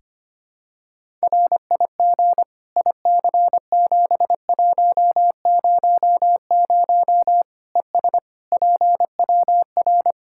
Morse code, heard as a keyed tone, about 25 wpm, 700 Hz. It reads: RIG IC7100 ES PWR
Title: Practice QSO - Exchange 2 - Info - 25wpm